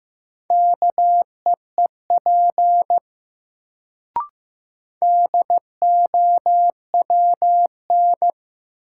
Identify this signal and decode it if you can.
Morse code, heard as a keyed tone, about 15 wpm, 700 Hz. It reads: KEEP DOWN
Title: Sentences from Top 300 Words - New Word Drill - 15wpm